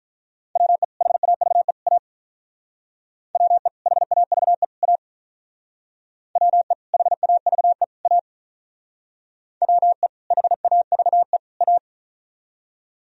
Morse code, heard as a keyed tone, about 47 wpm, 700 Hz. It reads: WE HAVE A WE HAVE A WE HAVE A WE HAVE A